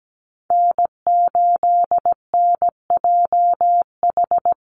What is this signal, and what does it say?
Morse code, heard as a keyed tone, about 17 wpm, 700 Hz. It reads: N8NJH
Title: USA Call Signs Repeated 3x 17wpm